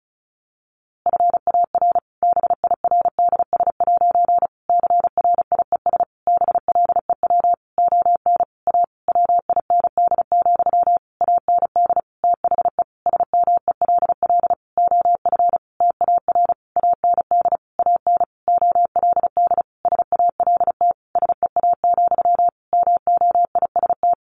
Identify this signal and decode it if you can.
Morse code, heard as a keyed tone, about 34 wpm, 700 Hz. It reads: FAR BIRDS' CRIES BLEW ON A WIND, AND THE SMELL OF TAR AND AN OLD SALT SEA, MOIST
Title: A Sound Of Thunder 35wpm